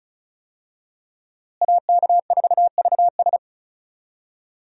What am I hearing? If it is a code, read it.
Morse code, heard as a keyed tone, about 36 wpm, 700 Hz. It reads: AK4VS
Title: USA Call Signs - 5 Characters - Repeated 3x - 35wpm